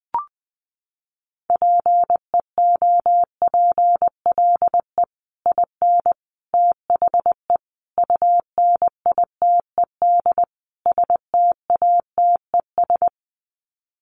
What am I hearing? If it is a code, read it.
Morse code, heard as a keyed tone, about 20 wpm, 700 Hz. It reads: PEOPLE IN THE UNITED STATES
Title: Sets of 5 Words Encore 20wpm